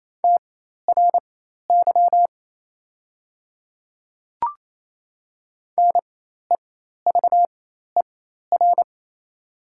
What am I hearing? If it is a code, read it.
Morse code, heard as a keyed tone, about 28 wpm, spaced as for 13 wpm, 700 Hz. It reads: TRY NEVER
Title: 4x Character Spacing - Top 500 Words - 28wpm